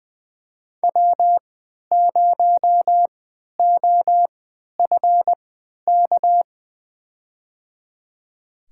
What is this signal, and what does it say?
Morse code, heard as a keyed tone, about 20 wpm, spaced as for 11 wpm, 700 Hz. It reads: W0OFK
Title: USA Call Signs - 5 Characters - 3x Character Spacing - 20wpm